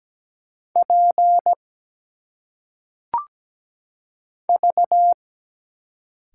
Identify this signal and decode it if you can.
Morse code, heard as a keyed tone, about 17 wpm, 700 Hz. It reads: P V
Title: Single Letters - Phonetic Pronunciation 17wpm